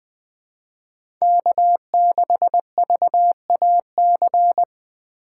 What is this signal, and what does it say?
Morse code, heard as a keyed tone, about 20 wpm, 700 Hz. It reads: K6VAC
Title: USA Call Signs - 5 Characters - 6x Character Spacing - 20wpm